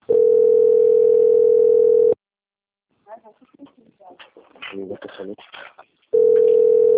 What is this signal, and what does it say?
Ringback tone (North America), 440 + 480 Hz, with speech.
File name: out- - - - - .30986